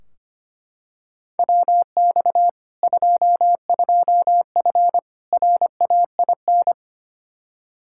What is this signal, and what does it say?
Morse code, heard as a keyed tone, about 25 wpm, 700 Hz. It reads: WX 22F RAIN
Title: Practice QSO - Exchange 2 - Info - 25wpm